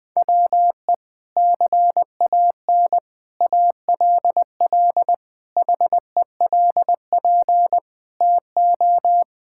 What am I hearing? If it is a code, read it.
Morse code, heard as a keyed tone, about 20 wpm, 700 Hz. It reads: WE CAN ALL HELP TO
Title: Sentences with 6-Word Phrases - 20wpm